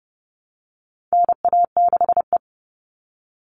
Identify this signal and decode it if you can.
Morse code, heard as a keyed tone, about 30 wpm, 700 Hz. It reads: NA6E
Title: All Cut Numbers - 4-Digit 30wpm